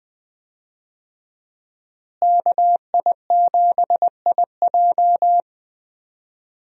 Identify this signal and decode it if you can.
Morse code, heard as a keyed tone, about 20 wpm, 700 Hz. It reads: KI7IJ
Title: USA Call Signs - 5 Characters - Repeated 3x - 20wpm